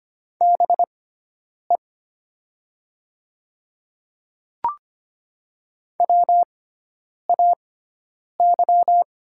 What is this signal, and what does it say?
Morse code, heard as a keyed tone, about 25 wpm, spaced as for 9 wpm, 700 Hz. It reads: BE WAY
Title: 6x Character Spacing - Top 200 Words - 25wpm